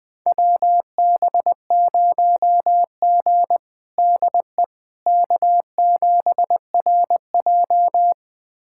Morse code, heard as a keyed tone, about 20 wpm, 700 Hz. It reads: WB0G DE K7RJ
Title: USA Callsign DE Callsign - 4 Characters - 20wpm